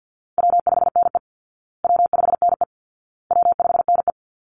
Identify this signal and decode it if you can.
Morse code, heard as a keyed tone, about 48 wpm, 700 Hz. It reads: W5NE W5NE W5NE
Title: Easy Call Signs Repeated 3x - 50wpm